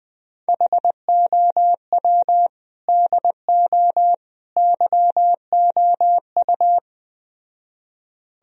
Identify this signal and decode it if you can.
Morse code, heard as a keyed tone, about 20 wpm, 700 Hz. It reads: HOW DO YOU
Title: Speed-Racing - Sets of 3 Words 17wpm